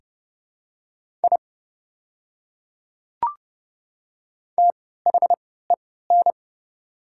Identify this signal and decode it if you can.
Morse code, heard as a keyed tone, about 30 wpm, spaced as for 17 wpm, 700 Hz. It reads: I THEN
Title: 3x Character Spacing - Top 100 Words - 30wpm